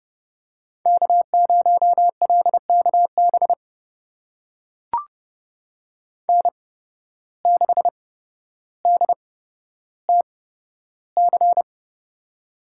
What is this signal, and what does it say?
Morse code, heard as a keyed tone, about 30 wpm, spaced as for 13 wpm, 700 Hz. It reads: K0LKB N6DTC
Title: USA Call Signs - 5 Characters - 8x Character Spacing - 30wpm